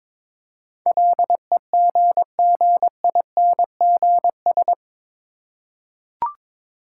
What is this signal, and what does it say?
Morse code, heard as a keyed tone, about 22 wpm, 700 Hz. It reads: LEGGINGS
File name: Difficult Words with Hints - 22wpm